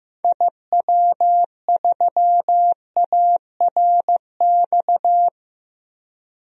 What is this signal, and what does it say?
Morse code, heard as a keyed tone, about 15 wpm, 700 Hz. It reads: IW3ARX
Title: DXCC-100 Call Signs - 6 Characters - Repeated 3x - 15wpm